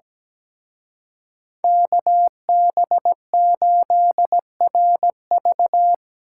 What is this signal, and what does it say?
Morse code, heard as a keyed tone, about 20 wpm, 700 Hz. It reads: KB8RV
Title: Speed Racing - USA 2x2 Call Signs 17wpm